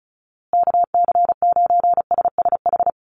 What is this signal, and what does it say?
Morse code, heard as a keyed tone, about 34 wpm, 700 Hz. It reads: KC9SSH
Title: USA Call Signs Repeated 2x 35wpm